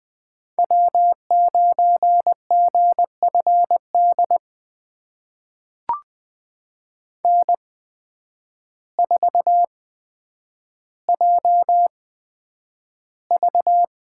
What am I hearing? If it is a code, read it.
Morse code, heard as a keyed tone, about 20 wpm, spaced as for 9 wpm, 700 Hz. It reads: W9GFD N4JV
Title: USA Call Signs - 5 Characters - 8x Character Spacing - 20wpm